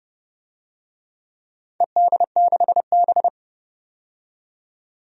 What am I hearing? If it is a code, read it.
Morse code, heard as a keyed tone, about 30 wpm, 700 Hz. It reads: ED6B
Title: DXCC-100 Call Signs - 4 Characters - 30wpm